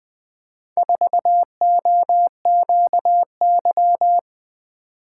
Morse code, heard as a keyed tone, about 21 wpm, 700 Hz. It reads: 4OQY
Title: Four Letter-Number Combinations - 20wpm